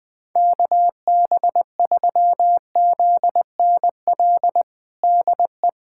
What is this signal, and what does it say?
Morse code, heard as a keyed tone, about 21 wpm, 700 Hz. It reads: KB3ZNL DE
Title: USA Callsign DE Callsign - 6 Characters - 20wpm